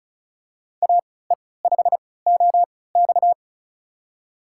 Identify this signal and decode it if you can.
Morse code, heard as a keyed tone, about 36 wpm, spaced as for 20 wpm, 700 Hz. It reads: AE5OX
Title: USA Call Signs - 5 Characters - 3x Character Spacing - 35wpm